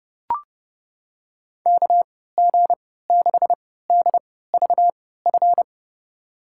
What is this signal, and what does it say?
Morse code, heard as a keyed tone, about 30 wpm, spaced as for 17 wpm, 700 Hz. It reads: KG6DVF
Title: USA Call Signs - 6 Characters - 3x Character Spacing - 30wpm